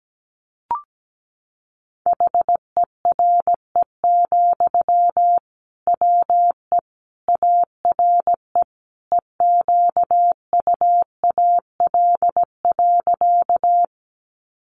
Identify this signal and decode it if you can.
Morse code, heard as a keyed tone, about 17 wpm, 700 Hz. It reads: HERE, WE ARE EQUAL.
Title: Sentences from Top 1400 Words 17wpm